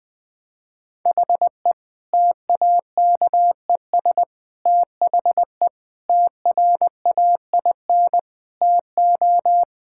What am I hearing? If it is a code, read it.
Morse code, heard as a keyed tone, about 20 wpm, 700 Hz. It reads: HE TAKES THE TRAIN TO
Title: Difficult Sentences with Hints - 20wpm